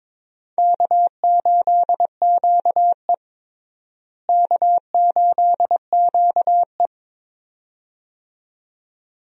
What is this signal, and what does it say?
Morse code, heard as a keyed tone, about 22 wpm, 700 Hz. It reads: K8QE K8QE
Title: USA Call Signs - 4 Characters - Repeated 2x - 22wpm